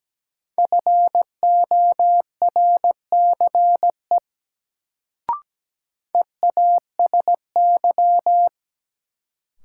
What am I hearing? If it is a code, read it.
Morse code, heard as a keyed tone, about 17 wpm, 700 Hz. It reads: FORCE EASY
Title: Sentences from Top 700 Words - New Word Drill - 17wpm